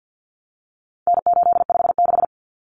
Morse code, heard as a keyed tone, about 50 wpm, 700 Hz. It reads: N856
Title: Common Cut Numbers - 4-Digit 50wpm